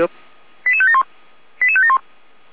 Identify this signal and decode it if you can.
Five-tone selective call (ZVEI), heard as quick runs of five tones, tones of about 70 ms, with speech. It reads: 88571 88571